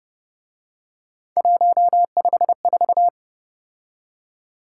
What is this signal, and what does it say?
Morse code, heard as a keyed tone, about 32 wpm, 700 Hz. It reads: 154